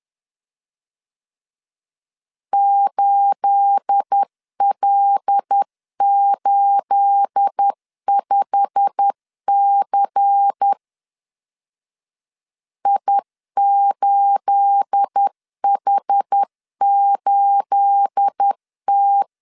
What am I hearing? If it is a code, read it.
Morse code, heard as a keyed tone, about 10 wpm, 800 Hz. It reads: 8L85C I8H8T
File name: LEZIONI / 9 28 10wpm 2842